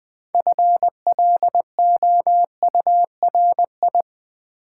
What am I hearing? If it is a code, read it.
Morse code, heard as a keyed tone, about 20 wpm, 700 Hz. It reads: FLOURI
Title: Build A Word - 3 Letters At A Time - 8-Letter Words - 20wpm